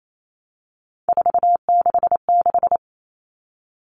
Morse code, heard as a keyed tone, about 28 wpm, 700 Hz. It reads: 466
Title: Numbers - 3-Digit 28wpm